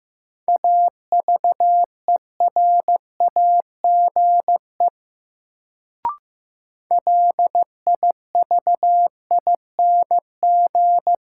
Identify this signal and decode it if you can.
Morse code, heard as a keyed tone, about 15 wpm, 700 Hz. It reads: AVERAGE LIVING
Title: Sentences from Top 1000 Words - Review Part 6 - New Word Drill - 15wpm